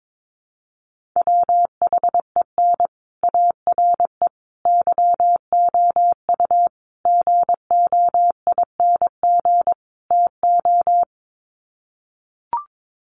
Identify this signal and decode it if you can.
Morse code, heard as a keyed tone, about 22 wpm, 700 Hz. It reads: WHEN ARE YOU GOING TO
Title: Sets of 5 Words Encore2 22wpm